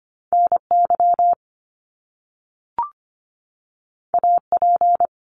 Morse code, heard as a keyed tone, about 25 wpm, 700 Hz. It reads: NY AP
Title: Top Two Letter Combinations in Words 25wpm